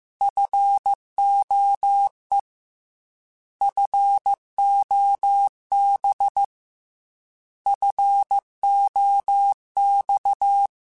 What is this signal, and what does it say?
Morse code, heard as a keyed tone, about 15 wpm, 800 Hz. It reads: FOE FOB FOX
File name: Alphabetical / Alphabetical mp3s / Fs @13